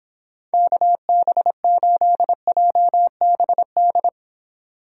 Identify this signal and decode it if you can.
Morse code, heard as a keyed tone, about 26 wpm, 700 Hz. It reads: KB8JBD